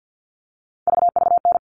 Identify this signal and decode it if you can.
Morse code, heard as a keyed tone, about 48 wpm, 700 Hz. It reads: VVN
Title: All Cut Numbers - 3-Digit 50wpm